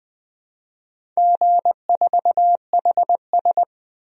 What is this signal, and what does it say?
Morse code, heard as a keyed tone, about 20 wpm, 700 Hz. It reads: G4HS
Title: DXCC-100 Call Signs - 4 Characters - 20wpm